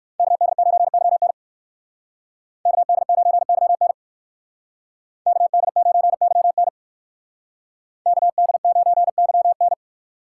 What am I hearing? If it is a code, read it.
Morse code, heard as a keyed tone, about 67 wpm, 700 Hz. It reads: KD9YN KD9YN KD9YN KD9YN